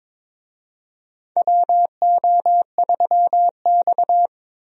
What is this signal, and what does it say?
Morse code, heard as a keyed tone, about 22 wpm, 700 Hz. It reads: WO3X
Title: USA Call Signs - 4 Characters - 22wpm